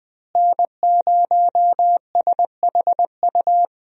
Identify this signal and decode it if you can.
Morse code, heard as a keyed tone, about 21 wpm, 700 Hz. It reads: N0SHU
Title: USA Call Signs - 5 Characters - Repeated 2x - 20wpm